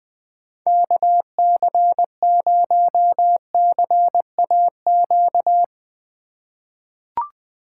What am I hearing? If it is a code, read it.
Morse code, heard as a keyed tone, about 20 wpm, 700 Hz. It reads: KC0CAQ
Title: USA Call Signs - 6 Characters - 3x Character Spacing - 20wpm